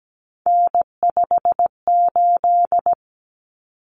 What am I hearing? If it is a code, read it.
Morse code, heard as a keyed tone, about 17 wpm, 700 Hz. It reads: N58
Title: Common Cut Numbers - 3-Digit 17wpm